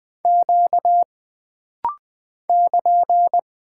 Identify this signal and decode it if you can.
Morse code, heard as a keyed tone, about 20 wpm, 700 Hz. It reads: Q <KN>
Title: Single Character (Letter:Number:Punctuation:Prosign) - Rapid-Fire - 20wpm